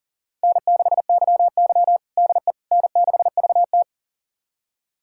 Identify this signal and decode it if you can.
Morse code, heard as a keyed tone, about 37 wpm, 700 Hz. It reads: N6YY DE N6VT
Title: USA Callsign DE Callsign - 4 Characters - 40wpm